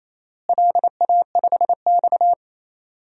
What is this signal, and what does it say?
Morse code, heard as a keyed tone, about 28 wpm, 700 Hz. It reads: LA5X
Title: DXCC-100 Call Signs - 4 Characters - Repeated 3x - 28wpm